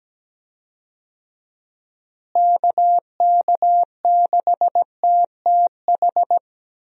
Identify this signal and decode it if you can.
Morse code, heard as a keyed tone, about 17 wpm, 700 Hz. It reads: KK6TTH
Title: USA Call Signs - 6 Characters - Repeated 2x - 17wpm